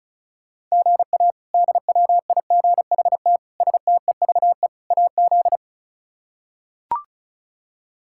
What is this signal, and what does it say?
Morse code, heard as a keyed tone, about 36 wpm, 700 Hz. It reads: GA DWIGHT STEVE AZ
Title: K1USN SST QSO Practice - 35wpm